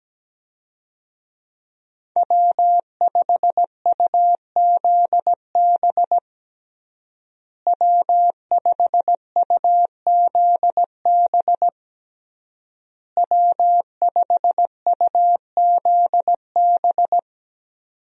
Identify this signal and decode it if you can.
Morse code, heard as a keyed tone, about 17 wpm, 700 Hz. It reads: W5UZB W5UZB W5UZB
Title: USA Call Signs - 5 Characters - Repeated 3x - 17wpm